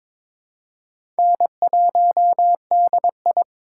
Morse code, heard as a keyed tone, about 22 wpm, 700 Hz. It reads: N1DI
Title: USA Call Signs - 4x Character Spacing - 22wpm